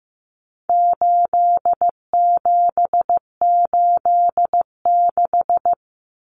Morse code, heard as a keyed tone, about 15 wpm, 700 Hz. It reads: 8786
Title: Numbers - 4-Digit 15wpm